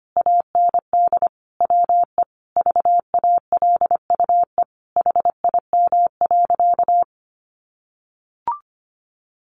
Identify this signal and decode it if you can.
Morse code, heard as a keyed tone, about 25 wpm, 700 Hz. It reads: AND WE VALUE HIM.
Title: Sentences from Top 600 Words 25wpm